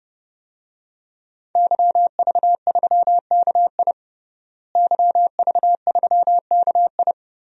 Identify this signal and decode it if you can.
Morse code, heard as a keyed tone, about 30 wpm, 700 Hz. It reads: YV3KI YV3KI
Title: DXCC-100 Call Signs - 5 Characters - Repeated 2x - 30wpm